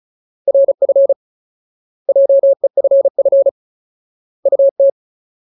Morse code, heard as a keyed tone, about 34 wpm, 550 Hz. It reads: RF JEFF UT
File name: Ultimate Word List / 35WPM Ultimate